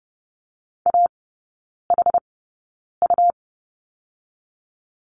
Morse code, heard as a keyed tone, about 30 wpm, spaced as for 9 wpm, 700 Hz. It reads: AHU
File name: Lesson 18e - Letter m - Three Characters 30wpm